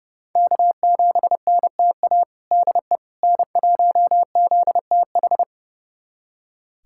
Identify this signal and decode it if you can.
Morse code, heard as a keyed tone, about 30 wpm, 700 Hz. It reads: K7NTA DE N1ZTH
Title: USA Callsign DE Callsign - 5 Characters - 30wpm